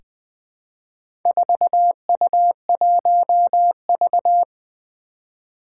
Morse code, heard as a keyed tone, about 20 wpm, 700 Hz. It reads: 4U1V
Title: Four Challenging Letter-Number Combinations 20wpm